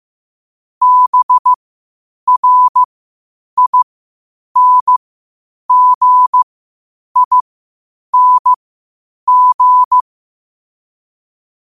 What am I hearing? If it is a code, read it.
Morse code, heard as a keyed tone, about 15 wpm, spaced as for 8 wpm, 1000 Hz. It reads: BRINGING
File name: qotdfiles / 8wpm